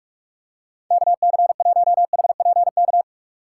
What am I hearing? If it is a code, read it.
Morse code, heard as a keyed tone, about 44 wpm, 700 Hz. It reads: KC1HPK
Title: USA Call Signs - 6 Characters - 3x Character Spacing - 45wpm